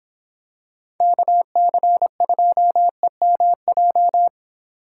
Morse code, heard as a keyed tone, about 26 wpm, 700 Hz. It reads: KC2EMJ